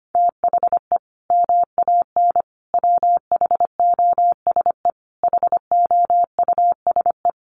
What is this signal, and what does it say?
Morse code, heard as a keyed tone, about 25 wpm, 700 Hz. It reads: THE MAN WHOSE HOUSE
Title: Sentences from Top 1200 Words 25wpm